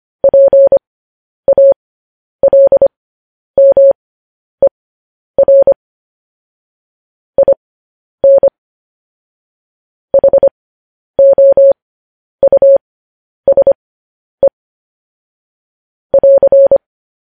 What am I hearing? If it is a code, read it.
Morse code, heard as a keyed tone, about 25 wpm, spaced as for 10 wpm, 550 Hz. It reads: PALMER IN HOUSE <AR>
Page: news headlines in morse code at 10 wpm